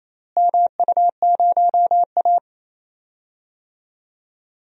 Morse code, heard as a keyed tone, about 28 wpm, 700 Hz. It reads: MU0A
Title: DXCC-100 Call Signs - 4 Characters - Repeated 2x - 28wpm